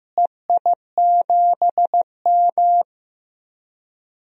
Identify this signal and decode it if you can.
Morse code, heard as a keyed tone, about 15 wpm, 700 Hz. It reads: EI7M
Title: DXCC-100 Call Signs - 4 Characters - Repeated 2x - 15wpm